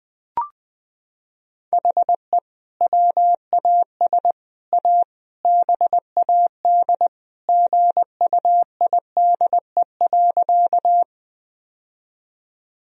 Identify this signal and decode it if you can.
Morse code, heard as a keyed tone, about 20 wpm, 700 Hz. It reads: HE WAS A BAD GUIDE.
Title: Sentences from Top 1000 Words - Review Part 3 - 20wpm